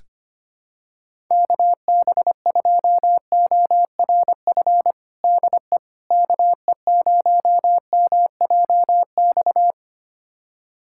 Morse code, heard as a keyed tone, about 25 wpm, 700 Hz. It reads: KB2ORF DE KE0MJX
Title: USA Callsign DE Callsign - 6 Characters - 25wpm